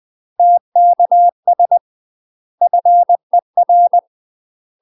Morse code, heard as a keyed tone, about 20 wpm, 700 Hz. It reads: TKS FER
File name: MASTER QSOs TRAINING/20WPM